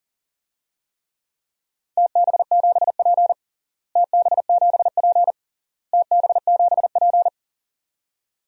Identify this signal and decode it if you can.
Morse code, heard as a keyed tone, about 40 wpm, 700 Hz. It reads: TB7P TB7P TB7P
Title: DXCC-100 Call Signs - 4 Characters - Repeated 3x - 40wpm